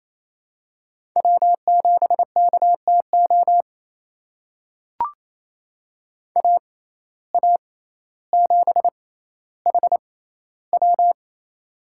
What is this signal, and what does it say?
Morse code, heard as a keyed tone, about 28 wpm, spaced as for 14 wpm, 700 Hz. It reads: W7KTO AA7HW
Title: USA Call Signs - 6x Character Spacing - 28wpm